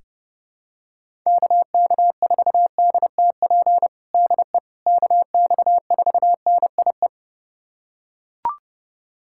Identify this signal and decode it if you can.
Morse code, heard as a keyed tone, about 30 wpm, 700 Hz. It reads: KK4DTP DE KX4NIE
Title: USA Callsign DE Callsign - 6 Characters - 30wpm